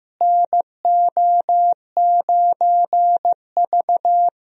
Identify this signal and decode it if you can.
Morse code, heard as a keyed tone, about 15 wpm, 700 Hz. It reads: NO9V
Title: USA Call Signs - 4 Characters - Repeated 2x - 15wpm